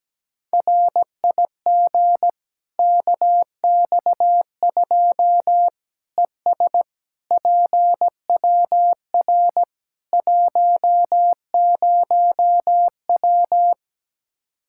Morse code, heard as a keyed tone, about 17 wpm, 700 Hz. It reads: RIG KX2 ES PWR 10W
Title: Practice QSO - Exchange 2 - Info - 17wpm